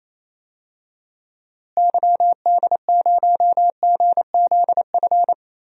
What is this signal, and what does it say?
Morse code, heard as a keyed tone, about 28 wpm, 700 Hz. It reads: YD0GZF
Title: DXCC-100 Call Signs - 6 Characters - Repeated 3x - 28wpm